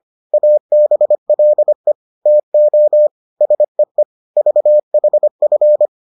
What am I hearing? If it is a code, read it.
Morse code, heard as a keyed tone, about 25 wpm, 600 Hz. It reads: ABLE TO SEE VHF